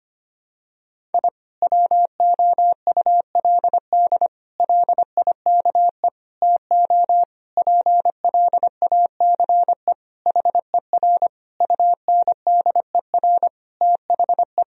Morse code, heard as a keyed tone, about 25 wpm, 700 Hz. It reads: I WOULD LIKE TO PLACE HER UNDER THE
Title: Sentences from Top 1000 Words - Review Part 1 - 25wpm